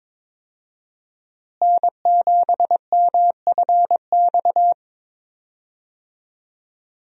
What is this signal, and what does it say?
Morse code, heard as a keyed tone, about 22 wpm, 700 Hz. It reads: N7MFX
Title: USA Call Signs - 5 Characters - Repeated 2x - 22wpm